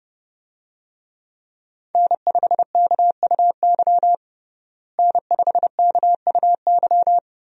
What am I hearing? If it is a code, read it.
Morse code, heard as a keyed tone, about 31 wpm, 700 Hz. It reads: N5KUY N5KUY
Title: USA Call Signs - 5 Characters - Repeated 3x - 30wpm